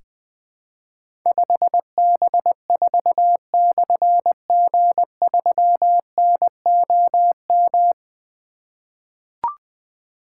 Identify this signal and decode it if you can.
Morse code, heard as a keyed tone, about 20 wpm, 700 Hz. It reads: 5B4/G3NOM
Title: Difficult Call Signs 20wpm